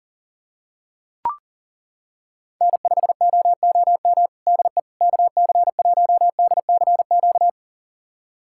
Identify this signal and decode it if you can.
Morse code, heard as a keyed tone, about 37 wpm, 700 Hz. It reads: N5OOM DE KC1DCQ
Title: USA Callsign DE Callsign 40wpm